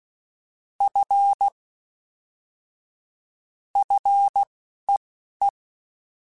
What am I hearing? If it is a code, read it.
Morse code, heard as a keyed tone, about 16 wpm, spaced as for 11 wpm, 800 Hz. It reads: F FEE
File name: Fs / Single F mp3s / Single F Prefixes 3x2 Twice @10